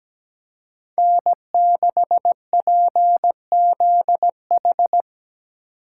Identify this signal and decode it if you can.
Morse code, heard as a keyed tone, about 19 wpm, 700 Hz. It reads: N6PZH
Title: Speed Racing - USA 1x3 Call Signs 17wpm